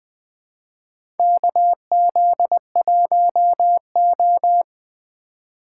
Morse code, heard as a keyed tone, about 20 wpm, 700 Hz. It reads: KZ1O
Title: USA Call Signs - 4 Characters - 2x Character Spacing - 20wpm